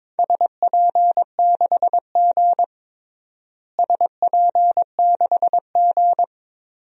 Morse code, heard as a keyed tone, about 22 wpm, 700 Hz. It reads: SP6G SP6G
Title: DXCC-100 Call Signs - 4 Characters - Repeated 3x - 22wpm